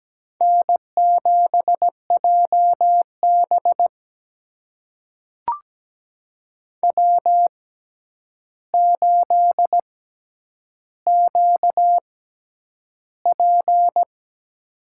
Morse code, heard as a keyed tone, about 17 wpm, spaced as for 9 wpm, 700 Hz. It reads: N7JB W8QP
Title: USA Call Signs - 4 Characters - 6x Character Spacing - 17wpm